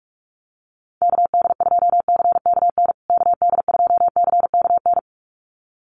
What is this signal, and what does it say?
Morse code, heard as a keyed tone, about 44 wpm, 700 Hz. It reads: KD2CKN KD2CKN
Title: USA Call Signs Repeated 2x 45wpm